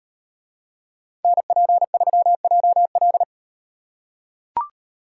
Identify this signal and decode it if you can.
Morse code, heard as a keyed tone, about 38 wpm, 700 Hz. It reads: NP3JL